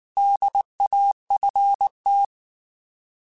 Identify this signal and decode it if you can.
Morse code, heard as a keyed tone, about 19 wpm, 800 Hz. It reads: DAFT
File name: Alphabetical / Alphabetical mp3s / Ds @18